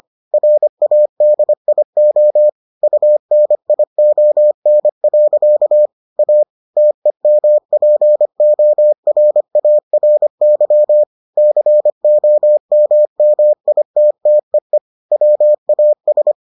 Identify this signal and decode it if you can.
Morse code, heard as a keyed tone, about 25 wpm, 600 Hz. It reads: RADIO UNION. A TEMPORARY COMMITTEE WAS